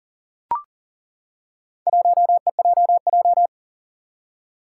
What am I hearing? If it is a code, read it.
Morse code, heard as a keyed tone, about 40 wpm, 700 Hz. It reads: 1EJJ
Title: Four Challenging Letter-Number Combinations 40wpm